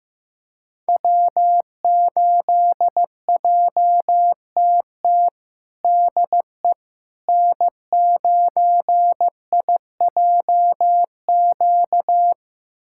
Morse code, heard as a keyed tone, about 15 wpm, 700 Hz. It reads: W8JTT DE N9IJQ
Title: USA Callsign DE Callsign - 5 Characters - 15wpm